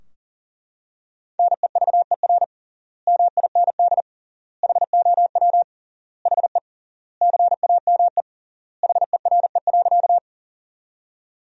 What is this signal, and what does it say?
Morse code, heard as a keyed tone, about 40 wpm, 700 Hz. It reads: NEVER MIND HOW HE CAME HERE.
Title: 3x Word-Spacing - Sentences from Top 500 Words - 40wpm